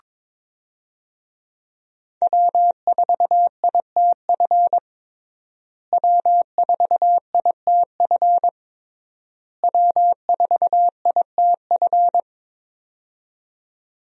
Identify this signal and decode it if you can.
Morse code, heard as a keyed tone, about 22 wpm, 700 Hz. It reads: W4ITF W4ITF W4ITF
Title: USA Call Signs - 5 Characters - Repeated 3x - 22wpm